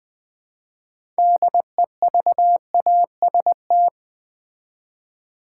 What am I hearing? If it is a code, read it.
Morse code, heard as a keyed tone, about 20 wpm, 700 Hz. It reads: DEVAST
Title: Build A Word - 3 Letters At A Time - 9-Letter Words - 20wpm